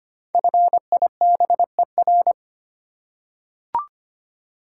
Morse code, heard as a keyed tone, about 25 wpm, 700 Hz. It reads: FIBER
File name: 5 Letter Words - First 3 Letters Given - 25wpm